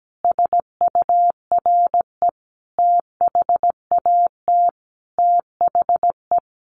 Morse code, heard as a keyed tone, about 17 wpm, 700 Hz. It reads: SURE THAT THE
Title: Sets of 5 Words 17wpm